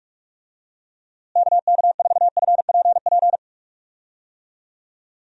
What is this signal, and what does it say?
Morse code, heard as a keyed tone, about 46 wpm, 700 Hz. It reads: KK4FPP
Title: USA Call Signs - 6 Characters - Repeated 3x - 45wpm